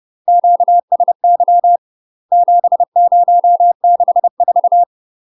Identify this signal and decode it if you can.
Morse code, heard as a keyed tone, about 30 wpm, 700 Hz. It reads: QSY 7064
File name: UK MASTER QSOs TRAINING/30WPM